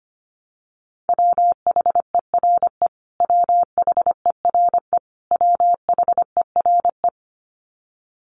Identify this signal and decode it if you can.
Morse code, heard as a keyed tone, about 25 wpm, 700 Hz. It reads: WHERE WHERE WHERE
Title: Top 200 Words Repeated 25wpm